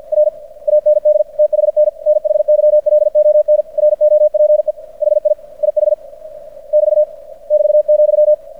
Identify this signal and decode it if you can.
Morse code, heard as a keyed tone, about 50 wpm, 600 Hz. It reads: R AND THN NHYLON ROPE HI IH <BT> <BT>,
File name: archived / CRSnet